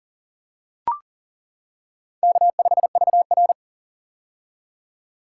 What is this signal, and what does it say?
Morse code, heard as a keyed tone, about 40 wpm, 700 Hz. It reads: K5VR
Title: K1USN SST QSO Practice - 40wpm